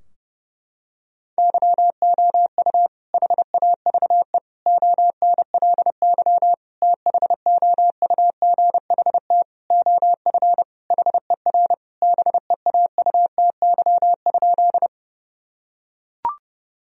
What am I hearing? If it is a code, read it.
Morse code, heard as a keyed tone, about 30 wpm, 700 Hz. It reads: YOU HAVE ONLY THOUGHT OF HER BEAUTY?
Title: Sentences from Top 1000 Words - Review Part 2 - 30wpm